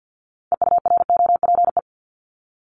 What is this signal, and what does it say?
Morse code, heard as a keyed tone, about 50 wpm, 700 Hz. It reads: EUROPE
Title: Multi-letter Phonemes 50wpm